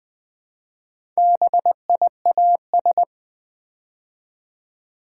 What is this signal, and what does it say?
Morse code, heard as a keyed tone, about 21 wpm, 690 Hz. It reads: BIAS
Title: Build A Word - 2 Letters At A Time - 4-Letter Words - 20wpm